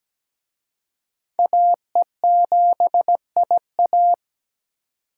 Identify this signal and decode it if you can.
Morse code, heard as a keyed tone, about 20 wpm, 700 Hz. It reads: AE7IA
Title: Speed Racing - USA 2x2 Call Signs 17wpm